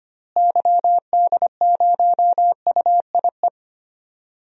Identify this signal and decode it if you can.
Morse code, heard as a keyed tone, about 25 wpm, 700 Hz. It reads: YD0UIE
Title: DXCC-100 Call Signs - 6 Characters - Repeated 2x - 25wpm